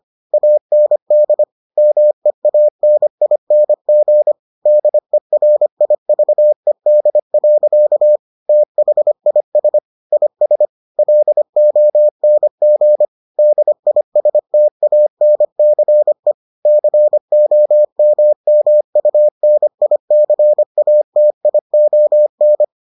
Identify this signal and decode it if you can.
Morse code, heard as a keyed tone, about 25 wpm, 600 Hz. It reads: AND MEANING DERIVED. THIS IS LONG DISTANCE COMMUNICATION